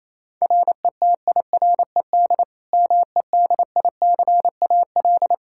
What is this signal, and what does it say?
Morse code, heard as a keyed tone, about 28 wpm, 700 Hz. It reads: RETIRED MEDICAL
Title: K1USN SST QSO Practice 28wpm